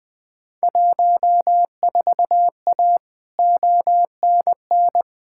Morse code, heard as a keyed tone, about 20 wpm, 700 Hz. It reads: 14A ONN
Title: Field Day Exchanges Encore - 20wpm